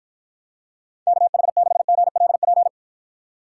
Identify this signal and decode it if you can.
Morse code, heard as a keyed tone, about 53 wpm, 700 Hz. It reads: KH6GLP